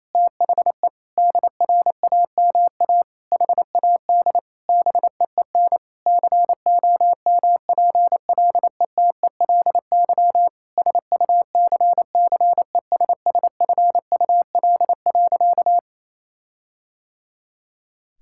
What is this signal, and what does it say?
Morse code, heard as a keyed tone, about 28 wpm, 700 Hz. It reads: THE DRAMA HAD BEEN COMPLETELY SUCCESSFUL.
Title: Sentences from Top 1600 Words 28wpm